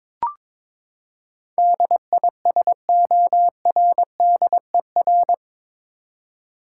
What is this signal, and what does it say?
Morse code, heard as a keyed tone, about 22 wpm, 700 Hz. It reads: DISORDER
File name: practice / 22wpm / Eight Letter Words - 22wpm